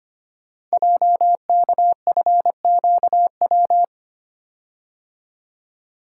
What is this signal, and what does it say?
Morse code, heard as a keyed tone, about 25 wpm, 700 Hz. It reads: JKFQW
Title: Code Groups - 5 Letters - 25wpm